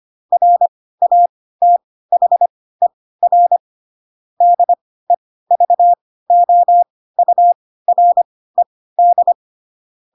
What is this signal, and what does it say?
Morse code, heard as a keyed tone, about 25 wpm, spaced as for 16 wpm, 700 Hz. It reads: RATHER DEVOURED
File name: Pinocchio / Pinocchio ch7 10 25 ik1hge